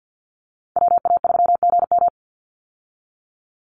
Morse code, heard as a keyed tone, about 50 wpm, 700 Hz. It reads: WA3GM
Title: Call Signs Repeated 2x - 50wpm